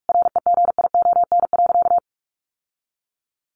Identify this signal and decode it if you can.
Morse code, heard as a keyed tone, about 44 wpm, 700 Hz. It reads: REGION.
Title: Sentences from Top 1500 Words 45wpm